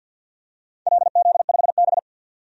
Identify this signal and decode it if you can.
Morse code, heard as a keyed tone, about 48 wpm, 700 Hz. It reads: LZ5B
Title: DXCC-100 Call Signs - 4 Characters - 50wpm